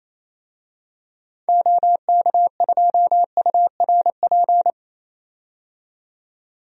Morse code, heard as a keyed tone, about 28 wpm, 700 Hz. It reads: OK2URP
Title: DXCC-100 Call Signs - 6 Characters - 28wpm